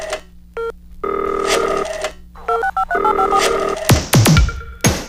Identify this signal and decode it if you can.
DTMF, heard as telephone touch tones, tones of about 90 ms. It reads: 65385